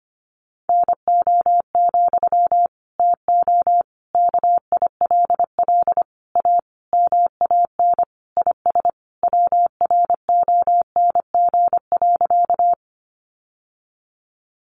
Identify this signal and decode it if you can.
Morse code, heard as a keyed tone, about 25 wpm, 700 Hz. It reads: NO, TO KILL A MAN IS WRONG.
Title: Sentences from Top 1400 Words 25wpm